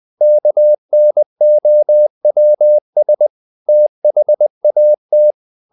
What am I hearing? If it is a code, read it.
Morse code, heard as a keyed tone, about 20 wpm, 600 Hz. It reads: KNOWS THAT
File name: the old man and the sea / the old man and the sea part 1 20wpm